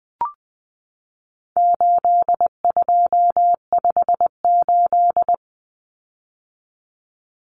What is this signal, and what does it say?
Morse code, heard as a keyed tone, about 20 wpm, 700 Hz. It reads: 8258
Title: Common Cut Numbers - 4-Digit 20wpm